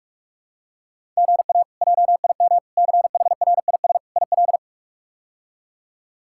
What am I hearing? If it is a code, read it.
Morse code, heard as a keyed tone, about 47 wpm, 700 Hz. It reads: GA JIM CHRIS IL
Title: K1USN SST QSO Practice - 45wpm